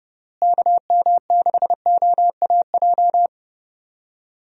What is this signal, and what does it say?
Morse code, heard as a keyed tone, about 30 wpm, 700 Hz. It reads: KM6OAJ
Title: USA Call Signs - 6 Characters - Repeated 3x - 30wpm